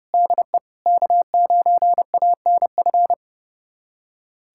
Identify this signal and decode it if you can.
Morse code, heard as a keyed tone, about 30 wpm, 700 Hz. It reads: DE K9ANF
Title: USA Callsign DE Callsign - 5 Characters - 30wpm